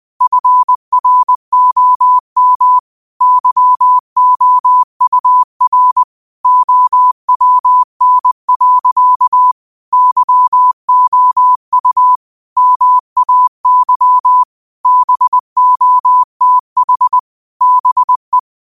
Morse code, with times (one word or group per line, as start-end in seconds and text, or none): 0.2-2.8 FROM
3.2-6.0 YOUR
6.4-9.5 OWN.
9.9-12.2 YOU
12.6-14.4 MAY
14.9-17.2 BOTH
17.6-18.4 BE